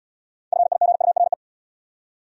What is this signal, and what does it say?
Morse code, heard as a keyed tone, about 70 wpm, 700 Hz. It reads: FIGURE